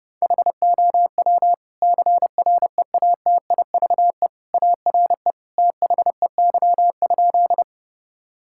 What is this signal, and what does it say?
Morse code, heard as a keyed tone, about 30 wpm, 700 Hz. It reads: HOW CREATIVE ARE THEY?
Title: Sentences from Top 1000 Words - Review Part 5 - 30wpm